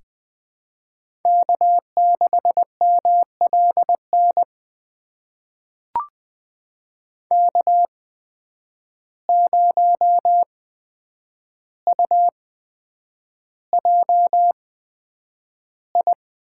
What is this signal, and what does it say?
Morse code, heard as a keyed tone, about 20 wpm, spaced as for 9 wpm, 700 Hz. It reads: K6MLN K0UJI
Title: USA Call Signs - 8x Character Spacing - 20wpm